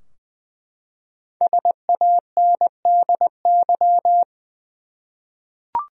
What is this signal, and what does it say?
Morse code, heard as a keyed tone, about 20 wpm, 700 Hz. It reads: SANDY
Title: USA Common Names - 20wpm